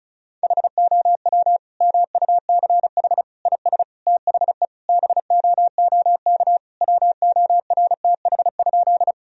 Morse code, heard as a keyed tone, about 36 wpm, 700 Hz. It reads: HOW MUCH IS THE BOOK WORTH?
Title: Sentences from Top 1000 Words - Review Part 1 - 35wpm